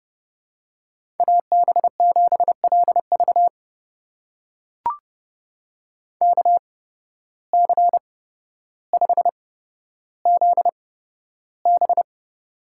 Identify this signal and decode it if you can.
Morse code, heard as a keyed tone, about 30 wpm, spaced as for 13 wpm, 700 Hz. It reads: AB7LV KC5ZB
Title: USA Call Signs - 5 Characters - 8x Character Spacing - 30wpm